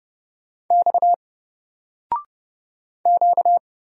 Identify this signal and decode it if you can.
Morse code, heard as a keyed tone, about 30 wpm, 700 Hz. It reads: X Q
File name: Single-Letter-Number-30wpm